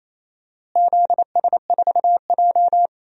Morse code, heard as a keyed tone, about 28 wpm, 700 Hz. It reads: ZS4J
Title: DXCC-100 Call Signs - 4 Characters - Repeated 3x - 28wpm